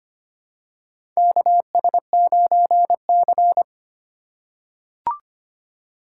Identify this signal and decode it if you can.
Morse code, heard as a keyed tone, about 25 wpm, 700 Hz. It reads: KS9C
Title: USA Call Signs - 3x Character Spacing - 25wpm